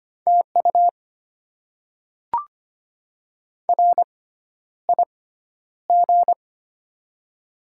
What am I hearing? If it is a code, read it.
Morse code, heard as a keyed tone, about 25 wpm, spaced as for 11 wpm, 700 Hz. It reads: TU RIG
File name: Top 20 QSO Elements - 6x Character Spacing - 25wpm